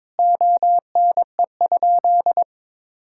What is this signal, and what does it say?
Morse code, heard as a keyed tone, about 22 wpm, 700 Hz. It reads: ONE?
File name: practice / 22wpm / Sentences with 3-Word Phrases - 22wpm